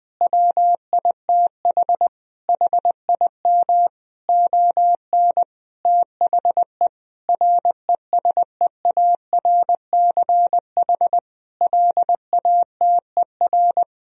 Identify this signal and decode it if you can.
Morse code, heard as a keyed tone, about 19 wpm, 700 Hz. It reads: WITH HIM ON THE RESEARCH LATER
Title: Sentences from Top 1000 Words - Review Part 4 - 20wpm